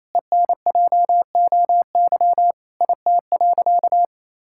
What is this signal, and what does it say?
Morse code, heard as a keyed tone, about 28 wpm, 700 Hz. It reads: ENJOY IT.
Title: Sentences from Top 1000 Words - Review Part 5 - 28wpm